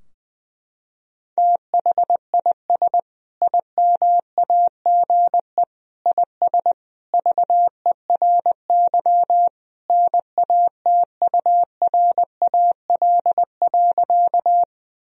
Morse code, heard as a keyed tone, about 21 wpm, 700 Hz. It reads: THIS IMAGE IS VERY NATURAL.
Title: Sentences from Top 1000 Words - Review Part 4 - 20wpm